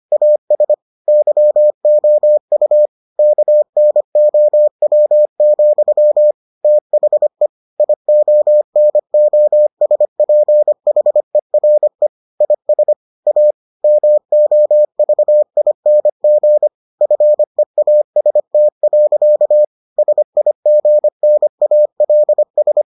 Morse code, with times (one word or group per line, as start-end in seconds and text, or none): 0.1-0.7 AS
1.1-2.9 YOU
3.2-6.3 KNOW,
6.6-7.5 THE
7.8-12.1 IONOSPHERE
12.4-12.9 IS
13.3-13.5 A
13.8-16.7 MOVING
17.0-19.6 FEAST.
20.0-22.8 SIGNALS